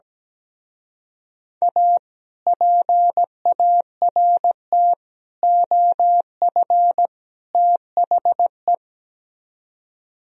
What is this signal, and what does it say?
Morse code, heard as a keyed tone, about 17 wpm, 700 Hz. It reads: A PART OF THE
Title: Speed-Racing - Sets of 4 Words 17wpm